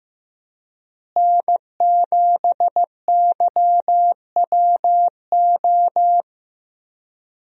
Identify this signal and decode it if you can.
Morse code, heard as a keyed tone, about 15 wpm, 700 Hz. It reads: N7YWO
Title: USA Call Signs - 5 Characters - Repeated 3x - 15wpm